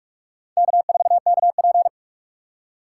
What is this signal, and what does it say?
Morse code, heard as a keyed tone, about 42 wpm, 700 Hz. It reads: K4KP
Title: USA Call Signs - 4 Characters - 3x Character Spacing - 45wpm